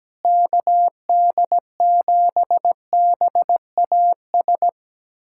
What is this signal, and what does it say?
Morse code, heard as a keyed tone, about 17 wpm, 700 Hz. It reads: KD7BAS
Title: USA Callsign DE Callsign - 6 Characters - 17wpm